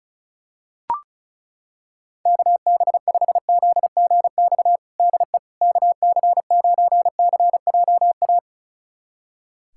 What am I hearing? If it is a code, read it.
Morse code, heard as a keyed tone, about 34 wpm, 700 Hz. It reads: KB5ZGX DE KC9CJA
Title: USA Callsign DE Callsign - 6 Characters - 35wpm